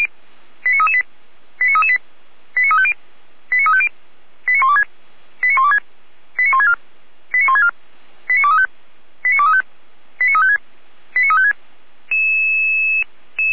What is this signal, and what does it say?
Five-tone selective call (ZVEI), heard as quick runs of five tones, tones of about 70 ms. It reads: 89338 89338 89350 89350 89126 89126 89164 89164 89235 89235 89356 89356